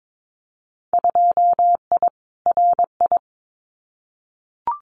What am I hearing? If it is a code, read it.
Morse code, heard as a keyed tone, about 22 wpm, 700 Hz. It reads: 2I RI
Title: Winter Field Day Exchanges Encore 22wpm